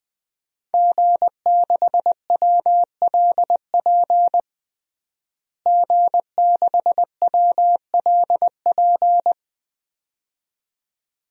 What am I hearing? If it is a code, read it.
Morse code, heard as a keyed tone, about 21 wpm, 700 Hz. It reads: G6WLP G6WLP
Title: DXCC-100 Call Signs - 5 Characters - Repeated 2x - 20wpm